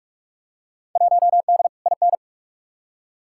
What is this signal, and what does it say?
Morse code, heard as a keyed tone, about 46 wpm, 700 Hz. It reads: 1D IN
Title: Field Day Exchanges - 45wpm